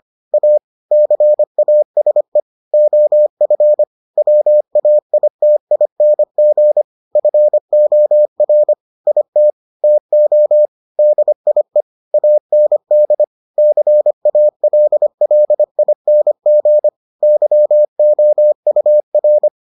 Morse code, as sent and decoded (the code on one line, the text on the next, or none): .- / -.-. .- ... / --- ..-. / .-- .- .. - .. -. --. / ..-. --- .-. / .. - / - --- / -.. .. / .- -. -.. / -.-. .- .-.. .-.. .. -. --. / -.-- --- ..- .-.
A CASE OF WAITING FOR IT TO DIE AND CALLING YOUR